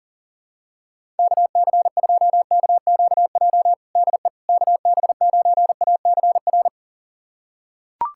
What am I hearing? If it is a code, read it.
Morse code, heard as a keyed tone, about 40 wpm, 700 Hz. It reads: KC2KQJ DE KB9ACR
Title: USA Callsign DE Callsign - 6 Characters - 40wpm